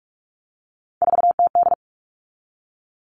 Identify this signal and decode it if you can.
Morse code, heard as a keyed tone, about 44 wpm, 700 Hz. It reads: 4TD
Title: All Cut Numbers - 3-Digit 45wpm